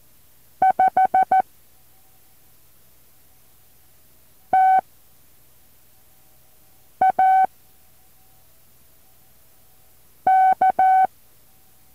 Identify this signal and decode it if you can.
Morse code, heard as a keyed tone, about 14 wpm, 760 Hz. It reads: 5 T A K
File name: intl morse code2